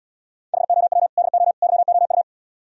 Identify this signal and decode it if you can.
Morse code, heard as a keyed tone, about 77 wpm, 700 Hz. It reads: HOW DO YOU